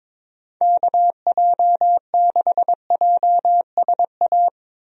Morse code, heard as a keyed tone, about 22 wpm, 700 Hz. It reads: KJ6JSA
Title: USA Call Signs - 6 Characters - Repeated 3x - 22wpm